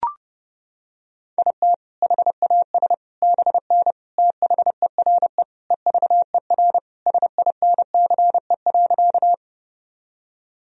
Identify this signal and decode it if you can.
Morse code, heard as a keyed tone, about 32 wpm, 700 Hz. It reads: IT HAS BN THERE EVER SINCE.
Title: Sentences from Top 400 Words and Abbreviations 30wpm